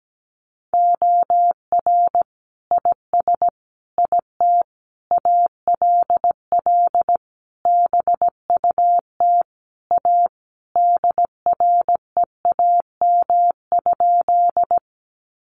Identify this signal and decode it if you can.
Morse code, heard as a keyed tone, about 17 wpm, 700 Hz. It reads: OR IS IT ALL BUT A DREAM?
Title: Sentences from Top 1500 Words 17wpm